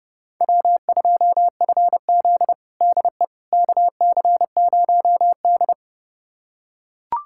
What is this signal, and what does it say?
Morse code, heard as a keyed tone, about 32 wpm, 700 Hz. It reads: W2FZ DE KC0D
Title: USA Callsign DE Callsign 30wpm